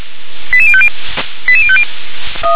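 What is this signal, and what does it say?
Five-tone selective call (ZVEI), heard as quick runs of five tones, tones of about 70 ms. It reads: 88050 88050
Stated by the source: ZVEI five-tone selective call received on an SDR.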